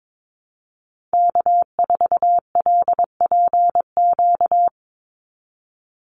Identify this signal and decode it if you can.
Morse code, heard as a keyed tone, about 22 wpm, 700 Hz. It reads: K4LPQ
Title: NA SOTA Activator Callsigns 22wpm